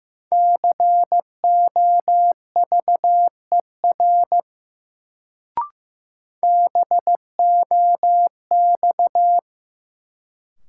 Morse code, heard as a keyed tone, about 15 wpm, 700 Hz. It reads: COVER BOX
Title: Sentences from Top 500 Words - New Word Drill - 15wpm